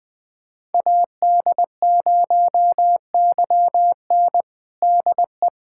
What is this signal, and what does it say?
Morse code, heard as a keyed tone, about 21 wpm, 700 Hz. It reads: AD0YN DE
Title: USA Callsign DE Callsign 20wpm